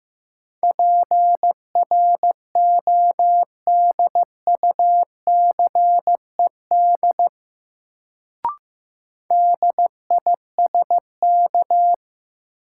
Morse code, heard as a keyed tone, about 15 wpm, 700 Hz. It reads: PRODUCED DISK
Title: Sentences from Top 1000 Words - Review Part 6 - New Word Drill - 15wpm